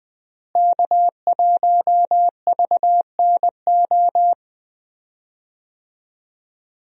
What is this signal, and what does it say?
Morse code, heard as a keyed tone, about 20 wpm, 700 Hz. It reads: K1VNO
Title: USA Call Signs - 5 Characters - Repeated 2x - 20wpm